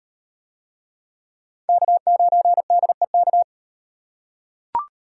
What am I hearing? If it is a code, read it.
Morse code, heard as a keyed tone, about 34 wpm, 700 Hz. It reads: K9DEK